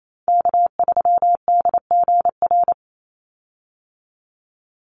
Morse code, heard as a keyed tone, about 28 wpm, 700 Hz. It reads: K3DGR
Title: Call Signs - 28wpm